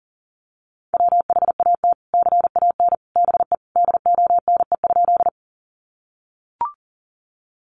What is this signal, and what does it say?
Morse code, heard as a keyed tone, about 40 wpm, 700 Hz. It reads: WHAT CAN BE DONE?
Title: Sentences from Top 500 Words 40wpm